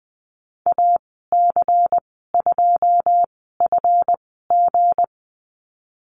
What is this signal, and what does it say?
Morse code, heard as a keyed tone, about 20 wpm, spaced as for 15 wpm, 700 Hz. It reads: AC2FG
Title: 186 Lesson 30g - Letter b - Call Signs 20wpm